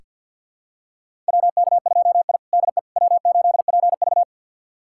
Morse code, heard as a keyed tone, about 45 wpm, 700 Hz. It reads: WK2I DE W8PV
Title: USA Callsign DE Callsign - 4 Characters - 50wpm